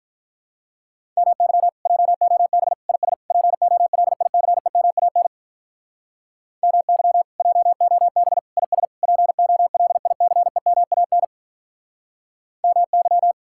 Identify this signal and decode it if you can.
Morse code, heard as a keyed tone, about 45 wpm, 700 Hz. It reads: MY JOB IS POLICEMAN MY JOB IS POLICEMAN MY